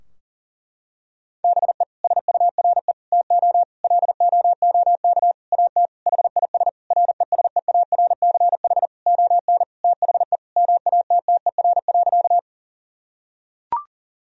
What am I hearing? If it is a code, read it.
Morse code, heard as a keyed tone, about 40 wpm, 700 Hz. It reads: BE SURE TO LOOK AT HIS RESEARCH ON THE MATTER.
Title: Sentences from Top 1000 Words - Review Part 4 - 40wpm